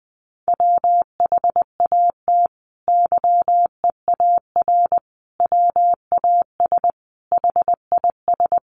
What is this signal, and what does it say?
Morse code, heard as a keyed tone, about 20 wpm, 700 Hz. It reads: WHAT YEAR WAS HIS
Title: Sentences from Top 800 Words 20wpm